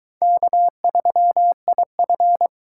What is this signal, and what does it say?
Morse code, heard as a keyed tone, about 22 wpm, 700 Hz. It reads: K3IF